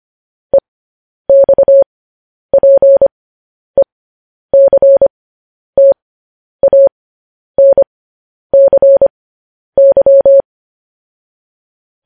Morse code, heard as a keyed tone, about 25 wpm, spaced as for 10 wpm, 550 Hz. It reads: EXPECTANCY
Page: News Headlines in Morse Code at 10 WPM